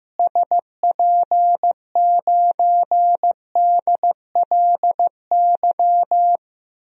Morse code, heard as a keyed tone, about 15 wpm, 700 Hz. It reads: SP9DLY
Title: DXCC-100 Call Signs - 6 Characters - Repeated 3x - 15wpm